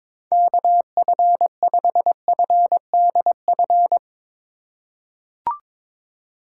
Morse code, heard as a keyed tone, about 22 wpm, 700 Hz. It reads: KF5FDF
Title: USA Call Signs - 6 Characters - 3x Character Spacing - 22wpm